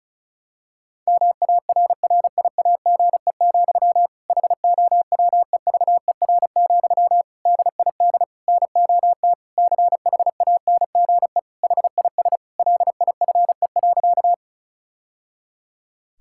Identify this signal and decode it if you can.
Morse code, heard as a keyed tone, about 34 wpm, 700 Hz. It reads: MARRIAGE, HOWEVER, DID NOT CHANGE HIS LIFE.
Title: Sentences from Top 1000 Words - Review Part 5 - 35wpm